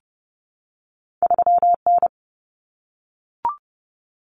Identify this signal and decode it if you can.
Morse code, heard as a keyed tone, about 30 wpm, 700 Hz. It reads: 3N
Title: Common Cut Numbers - 2-Digit 30wpm